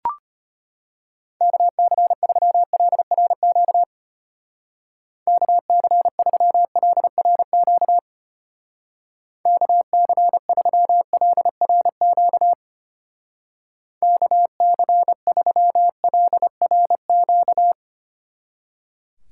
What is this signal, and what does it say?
Morse code, heard as a keyed tone, about 31 wpm, 700 Hz. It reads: KC3LRQ KC3LRQ KC3LRQ KC3LRQ
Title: Speed Racing - USA 2x3 Call Signs 25wpm